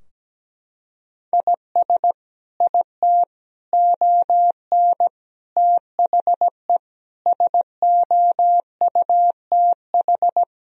Morse code, heard as a keyed tone, about 17 wpm, 700 Hz. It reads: IS IT ON THE SOUTH
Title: Sentences from Top 1000 Words - Review Part 2 - 17wpm